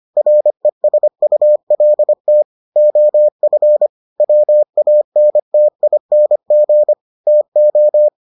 Morse code, heard as a keyed tone, about 25 wpm, 600 Hz. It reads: RESULT OF WANTING TO